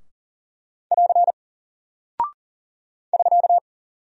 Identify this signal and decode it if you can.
Morse code, heard as a keyed tone, about 40 wpm, 700 Hz. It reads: <AR> <SK>
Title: Single Character (Letter:Number:Punctuation:Prosign) Alt 40wpm